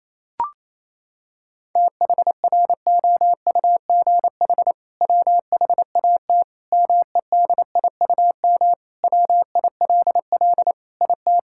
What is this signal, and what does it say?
Morse code, heard as a keyed tone, about 28 wpm, 700 Hz. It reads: THROUGH WHAT MEDIUM WILL IT
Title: Sentences from Top 1000 Words - Review Part 6 - 28wpm